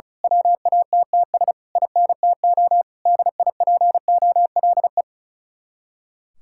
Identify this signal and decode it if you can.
Morse code, heard as a keyed tone, about 32 wpm, 700 Hz. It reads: WATTS INTO DIPOLE
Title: Speed-Racing Common QSO Phrases Encore - 35wpm